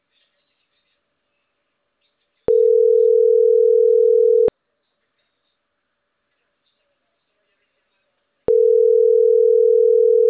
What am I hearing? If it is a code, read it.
Ringback tone (North America), 440 + 480 Hz.